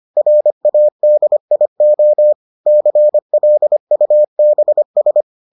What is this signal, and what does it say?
Morse code, heard as a keyed tone, about 25 wpm, 600 Hz. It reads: RADIO CLUBS